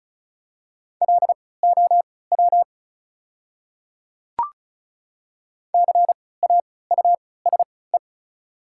Morse code, heard as a keyed tone, about 36 wpm, spaced as for 20 wpm, 700 Hz. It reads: LOW CAUSE
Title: 3x Character Spacing - Top 200 Words - 35wpm